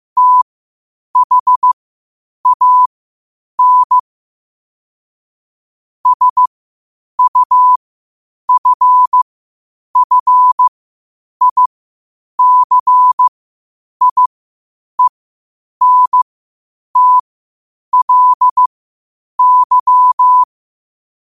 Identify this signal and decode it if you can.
Morse code, heard as a keyed tone, about 15 wpm, spaced as for 8 wpm, 1000 Hz. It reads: THAN SUFFICIENTLY
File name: qotdfiles / 8wpm